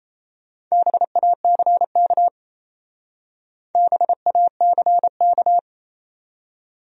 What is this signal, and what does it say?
Morse code, heard as a keyed tone, about 34 wpm, 700 Hz. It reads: BACK BACK
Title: Speed-Racing Top 300 Words 28wpm